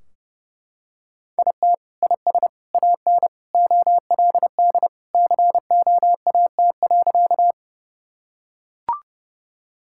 Morse code, heard as a keyed tone, about 30 wpm, 700 Hz. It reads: IT IS AN OLD COAT.
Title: Sentences from Top 1600 Words 30wpm